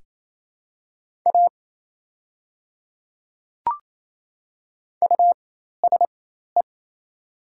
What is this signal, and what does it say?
Morse code, heard as a keyed tone, about 28 wpm, spaced as for 13 wpm, 700 Hz. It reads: A USE
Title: 4x Character Spacing - Top 200 Words - 28wpm